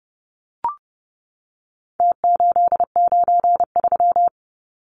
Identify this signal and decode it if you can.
Morse code, heard as a keyed tone, about 30 wpm, 700 Hz. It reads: T893
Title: Numbers - Contest Style - 4-Digit 30wpm